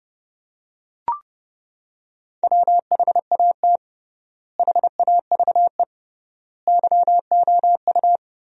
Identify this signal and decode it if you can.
Morse code, heard as a keyed tone, about 30 wpm, 700 Hz. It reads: WHAT HAVE YOU
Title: 3x Word-Spacing - Sentences from Top 100 Words - 30wpm